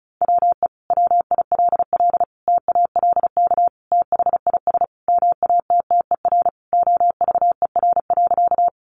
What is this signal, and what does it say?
Morse code, heard as a keyed tone, about 34 wpm, 700 Hz. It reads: WE WILL TALK THIS MATTER OVER.
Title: Sentences from Top 700 Words 35wpm